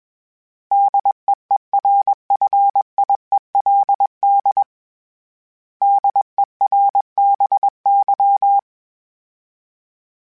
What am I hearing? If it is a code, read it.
Morse code, heard as a keyed tone, about 21 wpm, 800 Hz. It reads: DEERFIELD DERBY
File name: fromSMs / FromSMs mp3s / Cities@20